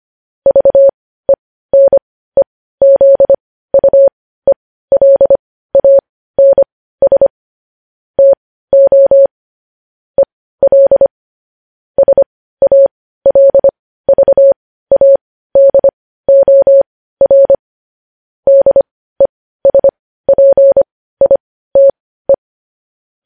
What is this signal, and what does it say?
Morse code, heard as a keyed tone, about 25 wpm, spaced as for 15 wpm, 550 Hz. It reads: VENEZUELANS TO EL SALVADOR DESPITE
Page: news-headlines-in-morse-code-at-15-wpm